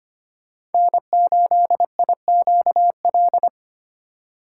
Morse code, heard as a keyed tone, about 25 wpm, 700 Hz. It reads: N8IQL
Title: USA Call Signs - 5 Characters - 2x Character Spacing - 25wpm